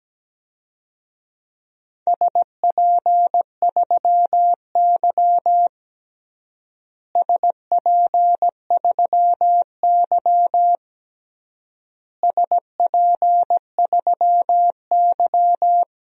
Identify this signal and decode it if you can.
Morse code, heard as a keyed tone, about 17 wpm, 700 Hz. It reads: SP3Y SP3Y SP3Y
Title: DXCC-100 Call Signs - 4 Characters - Repeated 3x - 17wpm